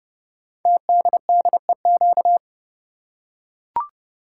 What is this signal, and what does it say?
Morse code, heard as a keyed tone, about 30 wpm, 700 Hz. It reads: TDDEQ
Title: Code Groups - 5 Letters - 30wpm